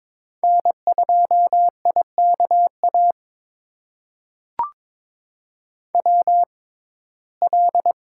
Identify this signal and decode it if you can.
Morse code, heard as a keyed tone, about 22 wpm, spaced as for 16 wpm, 700 Hz. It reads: N2IKA WL
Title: USA Call Signs - 6x Character Spacing - 22wpm